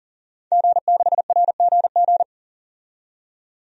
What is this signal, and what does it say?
Morse code, heard as a keyed tone, about 40 wpm, 700 Hz. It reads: G6RGG
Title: DXCC-100 Call Signs - 5 Characters - 40wpm